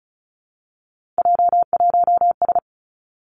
Morse code, heard as a keyed tone, about 36 wpm, 700 Hz. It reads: J1S
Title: Three Challenging Letter-Number Combinations 35wpm